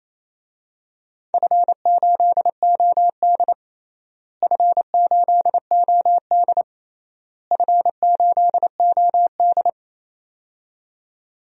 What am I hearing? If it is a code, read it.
Morse code, heard as a keyed tone, about 28 wpm, 700 Hz. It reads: F8OD F8OD F8OD
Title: DXCC-100 Call Signs - 4 Characters - Repeated 3x - 28wpm